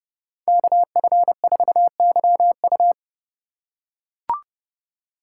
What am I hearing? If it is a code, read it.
Morse code, heard as a keyed tone, about 30 wpm, 700 Hz. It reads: KF4YU